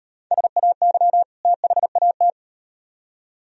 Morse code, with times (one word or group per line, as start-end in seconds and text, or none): 0.3-1.2 SAY
1.4-2.3 THAT